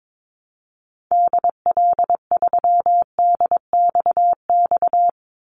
Morse code, heard as a keyed tone, about 22 wpm, 700 Hz. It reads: DL3DXX
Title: Call Signs Repeated 3x - 22wpm